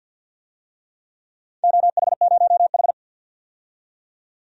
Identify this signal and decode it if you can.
Morse code, heard as a keyed tone, about 48 wpm, 700 Hz. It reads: OH0H
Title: DXCC-100 Call Signs - 4 Characters - 50wpm